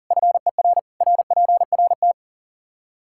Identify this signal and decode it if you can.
Morse code, heard as a keyed tone, about 40 wpm, 700 Hz. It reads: FER RPRT
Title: Practice QSO - Exchange 2 - Info - 40wpm